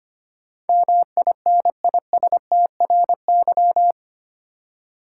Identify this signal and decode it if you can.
Morse code, heard as a keyed tone, about 25 wpm, 700 Hz. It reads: MINISTRY
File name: 8 Letter Words - First 3 Letters Given - 25wpm